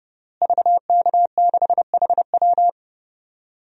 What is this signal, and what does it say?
Morse code, heard as a keyed tone, about 32 wpm, 700 Hz. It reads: VK6HW
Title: DXCC-100 Call Signs - 5 Characters - Repeated 2x - 30wpm